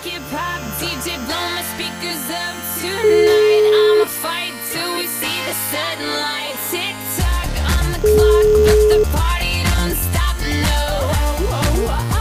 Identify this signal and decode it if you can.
Ringback tone (Europe, 425 Hz), with noise.